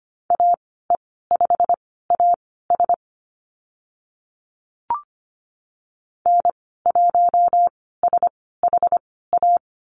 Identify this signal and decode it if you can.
Morse code, heard as a keyed tone, about 25 wpm, spaced as for 16 wpm, 700 Hz. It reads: AE5AS N1SHA
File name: Lesson 15g - Number 5 - Call Signs 25wpm